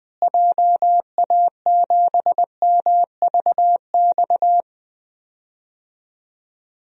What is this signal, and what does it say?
Morse code, heard as a keyed tone, about 20 wpm, 700 Hz. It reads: JA7MVX
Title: DXCC-100 Call Signs - 6 Characters - Repeated 3x - 20wpm